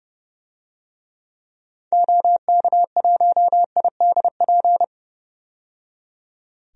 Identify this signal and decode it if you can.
Morse code, heard as a keyed tone, about 30 wpm, 700 Hz. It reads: OK1IDP
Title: DXCC-100 Call Signs - 6 Characters - 30wpm